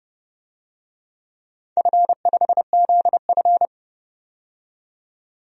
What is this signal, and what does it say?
Morse code, heard as a keyed tone, about 29 wpm, 700 Hz. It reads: F5ZF
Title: DXCC-100 Call Signs - 4 Characters - 30wpm